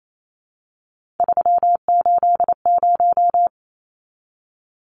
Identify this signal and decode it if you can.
Morse code, heard as a keyed tone, about 28 wpm, 700 Hz. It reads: 380
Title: Numbers - 3-Digit 28wpm